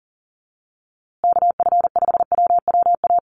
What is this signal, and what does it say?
Morse code, heard as a keyed tone, about 40 wpm, 700 Hz. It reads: KF5WWA
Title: USA Call Signs Repeated 3x 40wpm